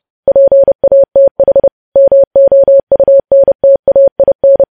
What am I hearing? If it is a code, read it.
Morse code, heard as a keyed tone, about 30 wpm, 550 Hz. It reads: PATH MOUNTAIN
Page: news headlines in morse code at 30 wpm